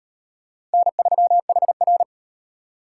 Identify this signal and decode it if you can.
Morse code, heard as a keyed tone, about 33 wpm, 700 Hz. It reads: N3HR